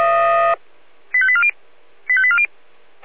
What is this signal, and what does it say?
Five-tone selective call (ZVEI), heard as quick runs of five tones, tones of about 70 ms. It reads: 85840 85840